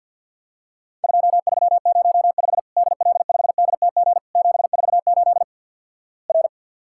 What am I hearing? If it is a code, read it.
Morse code, heard as a keyed tone, about 48 wpm, 700 Hz. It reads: 2305 DL5DTG 748 R
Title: Worked All Europe DX Contest - QTC Exchange - 50wpm